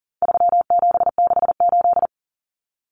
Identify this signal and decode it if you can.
Morse code, heard as a keyed tone, about 40 wpm, 700 Hz. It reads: 3768
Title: Numbers - 4-Digit 40wpm